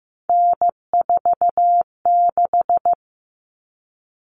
Morse code, heard as a keyed tone, about 15 wpm, 700 Hz. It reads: N46
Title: Common Cut Numbers - 3-Digit 15wpm